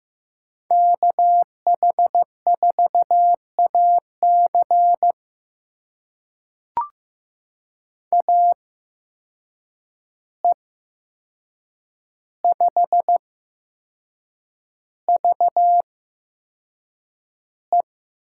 Morse code, heard as a keyed tone, about 15 wpm, spaced as for 6 wpm, 700 Hz. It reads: KH4AC AE5VE
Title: USA Call Signs - 5 Characters - 8x Character Spacing - 15wpm